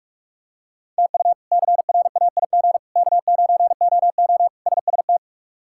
Morse code, heard as a keyed tone, about 46 wpm, 700 Hz. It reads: TU CRAIG K9OO SST
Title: K1USN SST QSO Practice - 45wpm